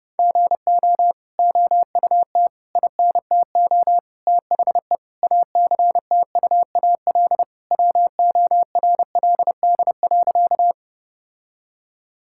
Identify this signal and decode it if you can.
Morse code, heard as a keyed tone, about 32 wpm, 700 Hz. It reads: GO OUT INTO THE ACTUAL WORLD.
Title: Sentences from Top 1000 Words - Review Part 5 - 30wpm